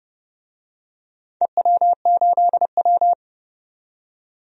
Morse code, heard as a keyed tone, about 31 wpm, 700 Hz. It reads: EW8W
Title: DXCC-100 Call Signs - 4 Characters - 30wpm